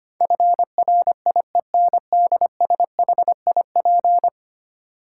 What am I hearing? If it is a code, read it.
Morse code, heard as a keyed tone, about 25 wpm, 700 Hz. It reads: FRIENDSHIP
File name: 10 Letter Words - First 3 Letters Given - 25wpm